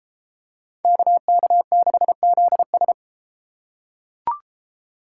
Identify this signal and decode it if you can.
Morse code, heard as a keyed tone, about 32 wpm, 700 Hz. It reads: KK6ZS